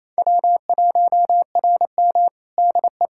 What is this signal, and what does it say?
Morse code, heard as a keyed tone, about 28 wpm, 700 Hz. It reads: W1RM DE
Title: Practice QSO - Exchange 4 - Ending - 28wpm